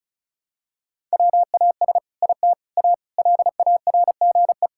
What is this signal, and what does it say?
Morse code, heard as a keyed tone, about 36 wpm, 700 Hz. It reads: WAS IT A LARGE
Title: Sentences from Top 1000 Words - Review Part 5 - 35wpm